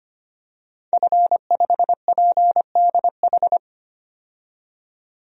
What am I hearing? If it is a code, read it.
Morse code, heard as a keyed tone, about 25 wpm, 700 Hz. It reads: F5PDH
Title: DXCC-100 Call Signs - 5 Characters - 25wpm